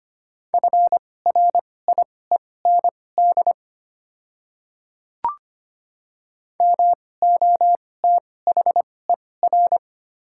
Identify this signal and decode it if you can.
Morse code, heard as a keyed tone, about 25 wpm, spaced as for 18 wpm, 700 Hz. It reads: FRIEND MOTHER
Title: 2x Character Spacing - Top 500 Words - 25wpm